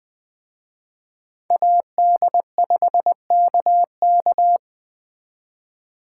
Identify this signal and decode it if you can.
Morse code, heard as a keyed tone, about 24 wpm, 700 Hz. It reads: AD5KK
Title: Speed Racing - USA 2x2 Call Signs 20wpm